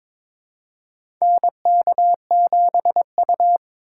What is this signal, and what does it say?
Morse code, heard as a keyed tone, about 22 wpm, 700 Hz. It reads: NK7U
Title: K1USN SST QSO Practice - 22wpm